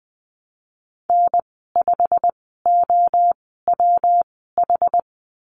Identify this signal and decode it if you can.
Morse code, heard as a keyed tone, about 20 wpm, spaced as for 15 wpm, 700 Hz. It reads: N5OWH
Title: 122 Lesson 20g - Number 3 - Call Signs 20wpm